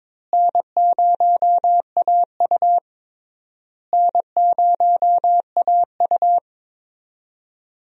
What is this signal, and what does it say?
Morse code, heard as a keyed tone, about 22 wpm, 700 Hz. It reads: N0AU N0AU
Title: USA Call Signs - 4 Characters - Repeated 3x - 22wpm